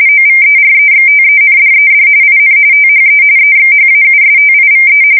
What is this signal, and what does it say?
RTTY, about 45.45 baud. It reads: OVEREXTENDED IN THE EXHAUSTIVE
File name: fsk170 / usnavy4